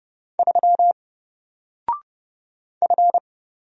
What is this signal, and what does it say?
Morse code, heard as a keyed tone, about 30 wpm, 700 Hz. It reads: 3 F
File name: Single-Letter-Number-30wpm